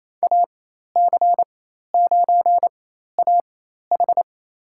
Morse code, heard as a keyed tone, about 28 wpm, spaced as for 13 wpm, 700 Hz. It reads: AC9AH
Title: USA Call Signs - 5 Characters - 4x Character Spacing - 28wpm